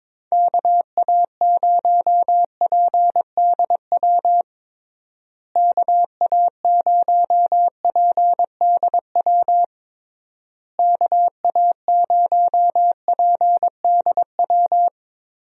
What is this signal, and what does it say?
Morse code, heard as a keyed tone, about 22 wpm, 700 Hz. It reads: KA0PDW KA0PDW KA0PDW
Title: USA Call Signs - 6 Characters - Repeated 3x - 22wpm